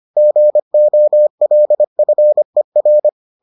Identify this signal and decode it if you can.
Morse code, heard as a keyed tone, about 25 wpm, 600 Hz. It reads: GOLFER